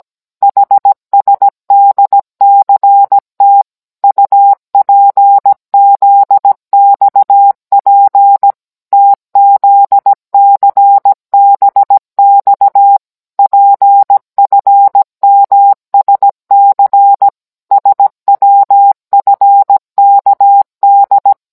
Morse code, heard as a keyed tone, about 17 wpm, 800 Hz. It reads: HSDCT UPZXP TZCBX PFMSC SWFKD